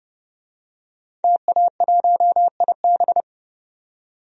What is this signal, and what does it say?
Morse code, heard as a keyed tone, about 30 wpm, 700 Hz. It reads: TA1IB
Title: DXCC-100 Call Signs - 5 Characters - 30wpm